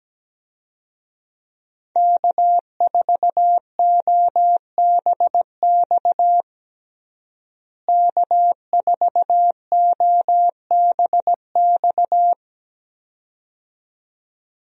Morse code, heard as a keyed tone, about 17 wpm, 700 Hz. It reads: K4OBX K4OBX
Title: USA Call Signs - 5 Characters - Repeated 2x - 17wpm